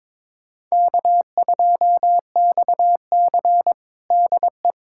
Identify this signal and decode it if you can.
Morse code, heard as a keyed tone, about 22 wpm, 700 Hz. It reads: K2XC DE
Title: USA Callsign DE Callsign - 4 Characters - 22wpm